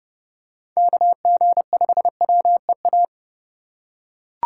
Morse code, heard as a keyed tone, about 30 wpm, 700 Hz. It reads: KG5WEA
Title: USA Call Signs - 6 Characters - 2x Character Spacing - 30wpm